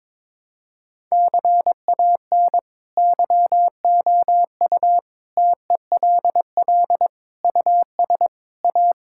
Morse code, heard as a keyed tone, about 22 wpm, 700 Hz. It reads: CAN YOU TELL US A
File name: practice / 22wpm / Sentences with 8-Word Phrases - 22wpm